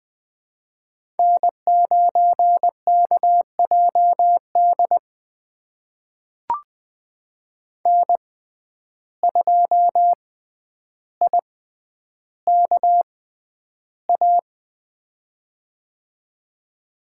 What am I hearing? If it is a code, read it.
Morse code, heard as a keyed tone, about 20 wpm, spaced as for 10 wpm, 700 Hz. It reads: N9KJD N2IKA
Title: USA Call Signs - 6x Character Spacing - 20wpm